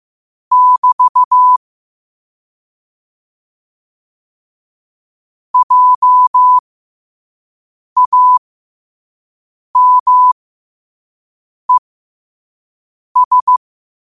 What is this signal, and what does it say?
Morse code, heard as a keyed tone, about 15 wpm, spaced as for 5 wpm, 1000 Hz. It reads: <BT> JAMES
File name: qotdfiles / 5wpm